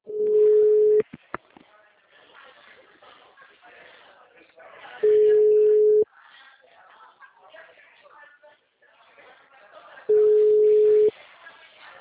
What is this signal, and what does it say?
Ringback tone (Europe, 425 Hz), with speech.